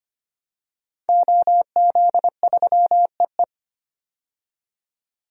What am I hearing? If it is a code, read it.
Morse code, heard as a keyed tone, about 25 wpm, 690 Hz. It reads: OZ3EE
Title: DXCC-100 Call Signs - 5 Characters - 25wpm